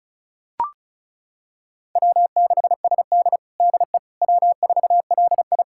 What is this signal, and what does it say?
Morse code, heard as a keyed tone, about 36 wpm, 700 Hz. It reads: W6SD DE W4LI
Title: USA Callsign DE Callsign - 4 Characters - 35wpm